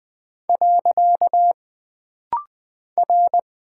Morse code, heard as a keyed tone, about 20 wpm, 700 Hz. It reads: . R
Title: Single Character (Letter:Number:Punctuation:Prosign) - Rapid-Fire - 20wpm